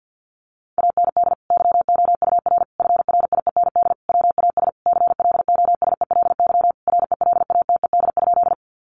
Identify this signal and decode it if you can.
Morse code, heard as a keyed tone, about 50 wpm, 700 Hz. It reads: AND YOUR FRIEND WAS CLOSELY RELATED?
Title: Sentences from Top 1400 Words 50wpm